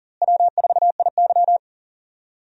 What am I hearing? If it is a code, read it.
Morse code, heard as a keyed tone, about 40 wpm, 700 Hz. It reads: W4IY
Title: USA Call Signs - 4 Characters - Repeated 3x - 40wpm